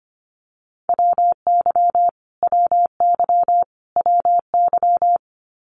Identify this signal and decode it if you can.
Morse code, heard as a keyed tone, about 25 wpm, 700 Hz. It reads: WY WY WY
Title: US States Repeated 3x 25wpm